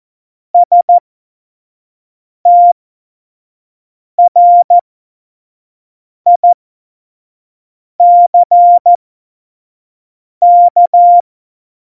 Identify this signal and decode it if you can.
Morse code, heard as a keyed tone, about 14 wpm, spaced as for 5 wpm, 700 Hz. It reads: STRICK
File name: Morsekurs / Witze / Witz 04. Februar 2020, 25 F70